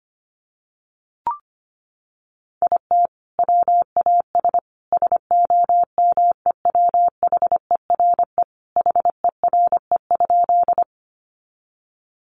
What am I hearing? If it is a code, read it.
Morse code, heard as a keyed tone, about 25 wpm, 700 Hz. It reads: IT WAS SOMEWHERE HERE?
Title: Sentences from Top 1400 Words 25wpm